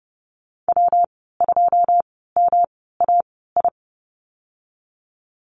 Morse code, heard as a keyed tone, about 30 wpm, 700 Hz. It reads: W 2 M A I
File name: Lesson 19g - Letter w - Call Signs 30wpm